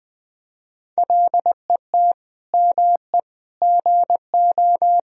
Morse code, heard as a keyed tone, about 20 wpm, 700 Hz. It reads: LET ME GO
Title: Sentences from Top 1000 Words - Review Part 4 - 20wpm